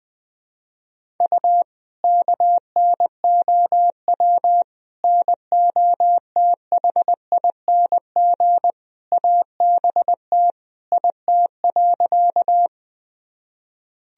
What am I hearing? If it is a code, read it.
Morse code, heard as a keyed tone, about 20 wpm, 700 Hz. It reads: U KNOW NOTHING ABT IT.
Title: Sentences from Top 500 Words and Abbreviations 20wpm